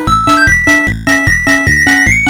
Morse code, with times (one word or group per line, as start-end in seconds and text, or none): none